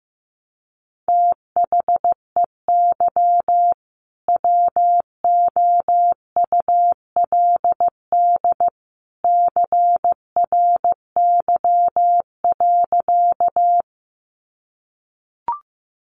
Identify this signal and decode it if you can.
Morse code, heard as a keyed tone, about 15 wpm, 700 Hz. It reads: THEY WOULD CRY.
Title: Sentences from Top 500 Words 15wpm